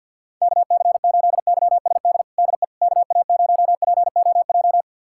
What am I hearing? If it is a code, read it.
Morse code, heard as a keyed tone, about 48 wpm, 700 Hz. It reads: KC8YSD DE KA0POJ
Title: USA Callsign DE Callsign - 6 Characters - 50wpm